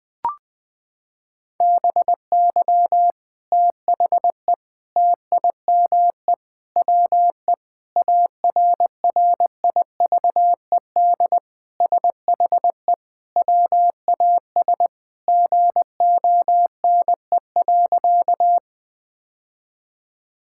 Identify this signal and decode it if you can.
Morse code, heard as a keyed tone, about 20 wpm, 700 Hz. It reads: BY THE TIME WE ARRIVED SHE WAS GONE.
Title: Sentences with 3-Word Phrases - 20wpm